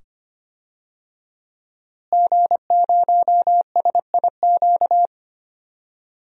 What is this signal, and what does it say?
Morse code, heard as a keyed tone, about 25 wpm, 700 Hz. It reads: G0SIQ
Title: DXCC-100 Call Signs - 5 Characters - Repeated 2x - 25wpm